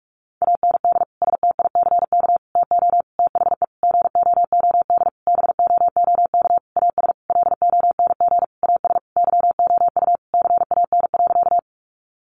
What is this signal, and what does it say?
Morse code, heard as a keyed tone, about 44 wpm, 700 Hz. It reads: AND STICK TO THE GOOD BOOK AS LONG AS YOU CAN.
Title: Sentences from Top 1200 Words 45wpm